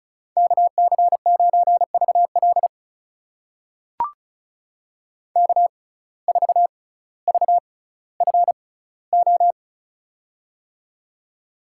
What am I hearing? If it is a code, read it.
Morse code, heard as a keyed tone, about 36 wpm, spaced as for 18 wpm, 700 Hz. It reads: KC9VL K4VFO
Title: USA Call Signs - 5 Characters - 6x Character Spacing - 35wpm